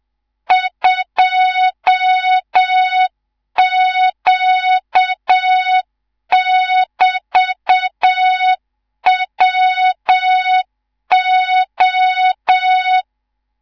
Morse code, heard as a keyed tone, about 7 wpm, 3700 Hz, with noise. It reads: 2Q<BT>WO
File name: Morze-HA2MN / hang / 19 GYAKORLAT